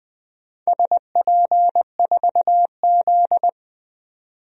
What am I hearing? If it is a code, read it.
Morse code, heard as a keyed tone, about 21 wpm, 700 Hz. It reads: SP4Z
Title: DXCC-100 Call Signs - 4 Characters - 20wpm